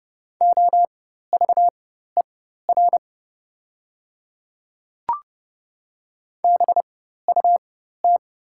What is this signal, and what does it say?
Morse code, heard as a keyed tone, about 30 wpm, spaced as for 14 wpm, 700 Hz. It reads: OVER BUT
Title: 4x Character Spacing - Top 100 Words - 30wpm